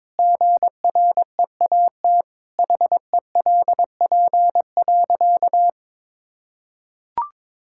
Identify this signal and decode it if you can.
Morse code, heard as a keyed tone, about 22 wpm, 700 Hz. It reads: GREAT HELP.
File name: practice / 22wpm / Sentences with 3-Word Phrases - 22wpm